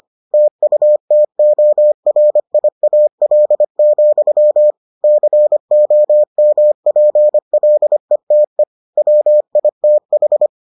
Morse code, heard as a keyed tone, about 25 wpm, 600 Hz. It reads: TUTORIAL, COMPLETE WITH